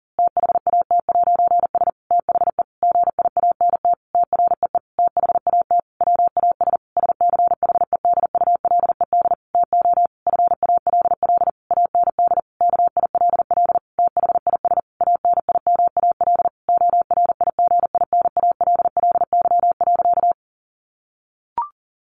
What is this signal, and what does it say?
Morse code, heard as a keyed tone, about 40 wpm, 700 Hz. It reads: THAT'S THE GIANT TREE THAT WAS SCHEDULED TO FALL AND KILL THIS ANIMAL ORIGINALLY.
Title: A Sound Of Thunder 40wpm